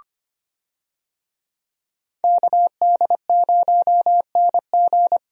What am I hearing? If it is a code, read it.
Morse code, heard as a keyed tone, about 25 wpm, 700 Hz. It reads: KD0NG
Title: USA Call Signs - 5 Characters - Repeated 3x - 25wpm